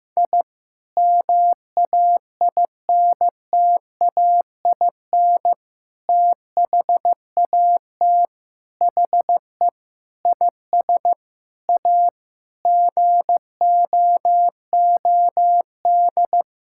Morse code, heard as a keyed tone, about 15 wpm, 700 Hz. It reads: I MAINTAIN THAT HE IS A GOOD
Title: Sentences from Top 1000 Words - Review Part 5 - 15wpm